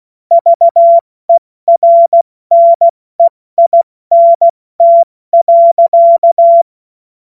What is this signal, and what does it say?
Morse code, heard as a keyed tone, about 16 wpm, 680 Hz. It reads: VERNEINT.
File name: Morsekurs / Witze / Witz 15. August 2023 70 F80